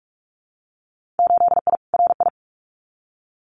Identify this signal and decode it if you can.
Morse code, heard as a keyed tone, about 46 wpm, 700 Hz. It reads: 8I RI
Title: Winter Field Day Exchanges Encore 45wpm